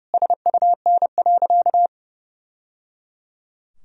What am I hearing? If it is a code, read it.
Morse code, heard as a keyed tone, about 30 wpm, 700 Hz. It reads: SUN.
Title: Sentences from Top 1000 Words - Review Part 2 - 30wpm